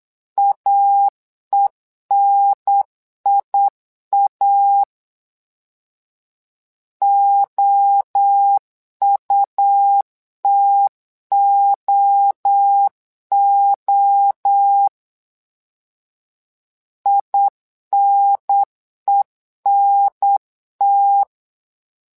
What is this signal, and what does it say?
Morse code, heard as a keyed tone, about 8 wpm, 800 Hz. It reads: AENIA OUTOO INENT
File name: LEZIONI / 3 MWRO / 3 MWRO 8wpm 2606